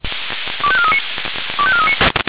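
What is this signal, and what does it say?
Five-tone selective call (ZVEI), heard as quick runs of five tones, tones of about 80 ms. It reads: 25439 25439
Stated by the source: Five-tone ZVEI selective call 25418 received on an SDR.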